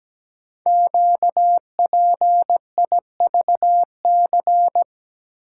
Morse code, heard as a keyed tone, about 17 wpm, 700 Hz. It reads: QPIVC
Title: Code Groups - 5 Letters - 17wpm